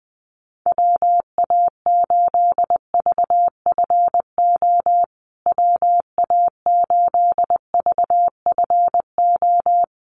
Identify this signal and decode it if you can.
Morse code, heard as a keyed tone, about 20 wpm, 700 Hz. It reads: WA8VFO WA8VFO
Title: USA Call Signs Repeated 3x 20wpm